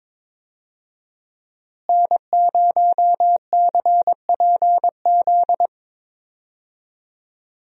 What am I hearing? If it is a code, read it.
Morse code, heard as a keyed tone, about 22 wpm, 700 Hz. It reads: N0CPZ
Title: USA Call Signs - 5 Characters - Repeated 3x - 22wpm